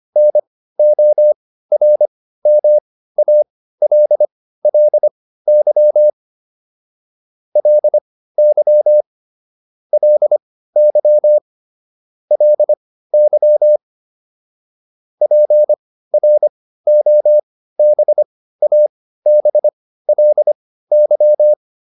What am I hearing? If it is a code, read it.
Morse code, heard as a keyed tone, about 25 wpm, spaced as for 15 wpm, 600 Hz. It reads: NORMALLY LY LY LY PROBABLY